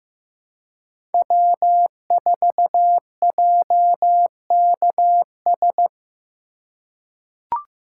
Morse code, heard as a keyed tone, about 15 wpm, 700 Hz. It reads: W4JKS
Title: USA Call Signs - 5 Characters - 3x Character Spacing - 15wpm